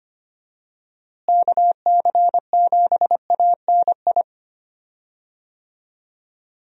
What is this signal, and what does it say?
Morse code, heard as a keyed tone, about 25 wpm, 700 Hz. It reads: KC7ANI
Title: USA Call Signs - 6 Characters - Repeated 3x - 25wpm